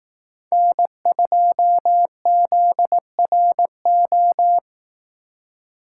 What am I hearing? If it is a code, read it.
Morse code, heard as a keyed tone, about 18 wpm, 700 Hz. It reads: N2ZRO
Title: Speed Racing - USA 1x3 Call Signs 15wpm